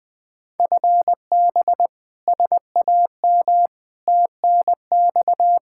Morse code, heard as a keyed tone, about 20 wpm, 700 Hz. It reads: FB SAM TNX
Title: Practice QSO - Exchange 3 - More Info - 20wpm